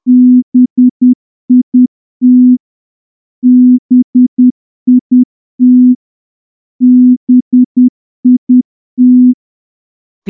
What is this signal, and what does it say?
Morse code, heard as a keyed tone, about 10 wpm, 250 Hz. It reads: BIT BIT BIT
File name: morse-training-10WPM